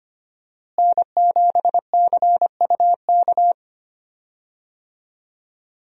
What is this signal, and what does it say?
Morse code, heard as a keyed tone, about 25 wpm, 700 Hz. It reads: N7CUK
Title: USA Call Signs - 5 Characters - Repeated 2x - 25wpm